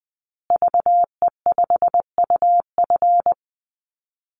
Morse code, heard as a keyed tone, about 20 wpm, 700 Hz. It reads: VE5UF
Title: Call Signs Repeated 3x - 20wpm